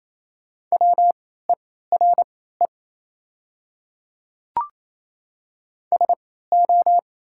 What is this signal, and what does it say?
Morse code, heard as a keyed tone, about 28 wpm, spaced as for 16 wpm, 700 Hz. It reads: WERE SO
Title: 3x Character Spacing - Top 100 Words - 28wpm